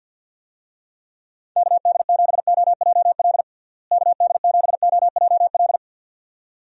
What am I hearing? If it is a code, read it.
Morse code, heard as a keyed tone, about 51 wpm, 700 Hz. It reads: KD7OJL KD7OJL
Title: USA Call Signs - 6 Characters - Repeated 2x - 50wpm